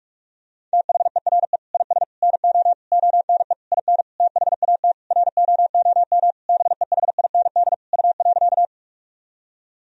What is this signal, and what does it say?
Morse code, heard as a keyed tone, about 46 wpm, 700 Hz. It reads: THERE IS NO ONE IN THAT ROOM BEHIND U.
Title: Sentences from Top 500 Words and Abbreviations 45wpm